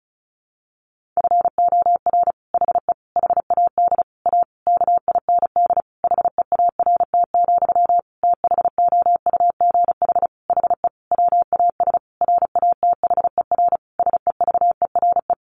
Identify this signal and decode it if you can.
Morse code, heard as a keyed tone, about 34 wpm, 700 Hz. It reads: FOR HE HAD A KIND HEART, THOUGH HE WAS RATHER SEVERE
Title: Sentences from Top 1400 Words 35wpm